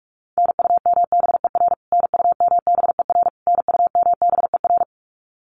Morse code, heard as a keyed tone, about 44 wpm, 700 Hz. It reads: NUMBER NUMBER NUMBER
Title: Top 500 Words Repeated 45wpm